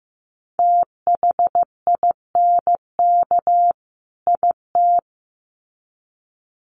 Morse code, heard as a keyed tone, about 15 wpm, 700 Hz. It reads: THINK IT
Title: Sets of 2 Words 15wpm